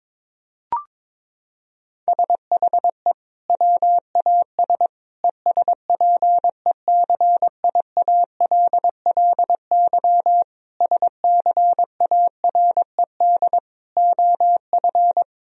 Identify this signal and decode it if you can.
Morse code, heard as a keyed tone, about 22 wpm, 700 Hz. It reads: SHE WAS ESPECIALLY SCARED OF
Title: Sentences from Top 1000 Words - Review Part 4 - 22wpm